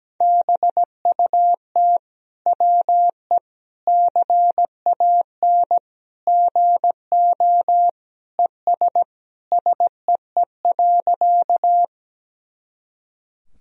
Morse code, heard as a keyed tone, about 17 wpm, 700 Hz. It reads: BUT WE CAN GO ES SEE.
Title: Sentences from Top 100 Words and Abbreviations 17wpm